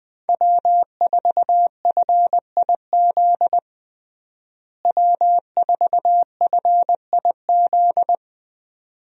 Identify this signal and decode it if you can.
Morse code, heard as a keyed tone, about 20 wpm, 700 Hz. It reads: W4FIZ W4FIZ
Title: USA Call Signs - 5 Characters - Repeated 2x - 20wpm